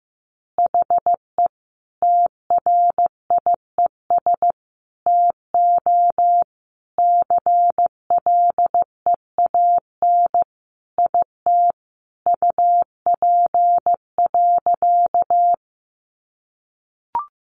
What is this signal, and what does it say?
Morse code, heard as a keyed tone, about 15 wpm, 700 Hz. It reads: HE TRIES TO CLEAN IT UP.
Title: Additional Easy Phrases 15wpm